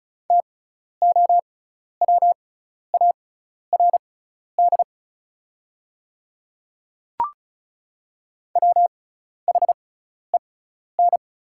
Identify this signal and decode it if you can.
Morse code, heard as a keyed tone, about 34 wpm, spaced as for 12 wpm, 700 Hz. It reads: TOWARD WHEN
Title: 6x Character Spacing - Top 500 Words - 35wpm